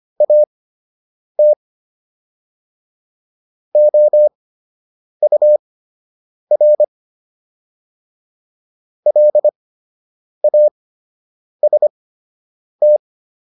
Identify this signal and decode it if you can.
Morse code, heard as a keyed tone, about 25 wpm, spaced as for 8 wpm, 600 Hz. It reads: AT OUR LAST